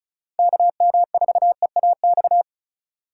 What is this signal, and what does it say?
Morse code, heard as a keyed tone, about 36 wpm, 700 Hz. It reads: KM4EAX
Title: USA Call Signs - 6 Characters - 2x Character Spacing - 35wpm